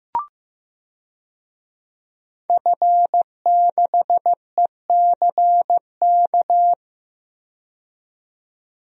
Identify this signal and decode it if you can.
Morse code, heard as a keyed tone, about 15 wpm, 700 Hz. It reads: F6ECK
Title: DXCC-100 Call Signs - 5 Characters - 15wpm